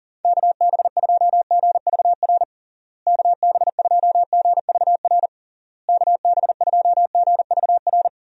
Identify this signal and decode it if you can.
Morse code, heard as a keyed tone, about 37 wpm, 700 Hz. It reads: KB2GVR KB2GVR KB2GVR
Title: USA Call Signs - 6 Characters - Repeated 3x - 40wpm